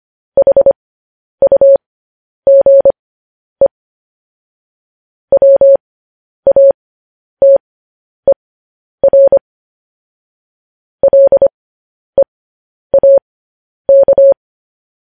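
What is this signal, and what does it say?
Morse code, heard as a keyed tone, about 25 wpm, spaced as for 10 wpm, 550 Hz. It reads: HUGE WATER LEAK
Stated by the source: Morse code practice sending news headlines at 10 wpm.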